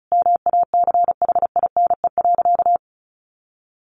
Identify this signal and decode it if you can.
Morse code, heard as a keyed tone, about 34 wpm, 700 Hz. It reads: MACHINE.
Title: A Sound Of Thunder 35wpm